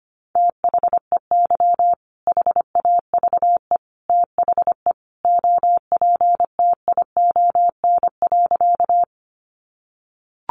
Morse code, heard as a keyed tone, about 25 wpm, 700 Hz. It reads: THEY HAVE THE OPTION.
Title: Sentences from Top 1300 Words 25wpm